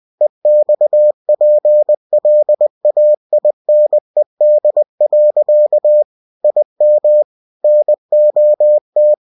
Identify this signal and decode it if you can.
Morse code, heard as a keyed tone, about 20 wpm, 600 Hz. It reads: EXPLAINED. IM NOT
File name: cw-books / winnie-the-pooh / winnie-20wpm-page-0045